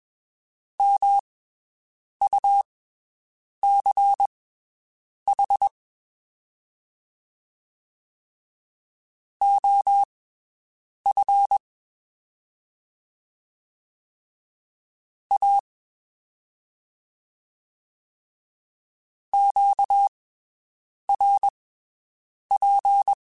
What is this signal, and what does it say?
Morse code, heard as a keyed tone, about 21 wpm, spaced as for 7 wpm, 800 Hz. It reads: MUCH OF A QRP
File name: (spaced) / QSOs Spaced-mp3s / WD0BGP 1 @20 sp